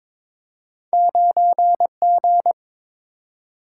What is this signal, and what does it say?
Morse code, heard as a keyed tone, about 22 wpm, 700 Hz. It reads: 9G
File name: practice / 22wpm / Two Letter-Number Combinations - 22wpm